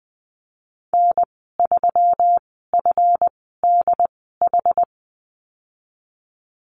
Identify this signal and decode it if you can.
Morse code, heard as a keyed tone, about 20 wpm, spaced as for 15 wpm, 700 Hz. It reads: N3FDH
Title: 161 Lesson 26g - Letter g - Call Signs 20wpm